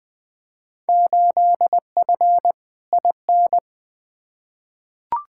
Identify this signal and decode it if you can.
Morse code, heard as a keyed tone, about 20 wpm, 700 Hz. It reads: 8F IN
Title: Field Day Exchanges Encore - 20wpm